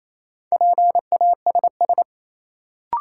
Morse code, heard as a keyed tone, about 28 wpm, 700 Hz. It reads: PASS
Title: Sentences from Top 400 Words - New Word Drill - 28wpm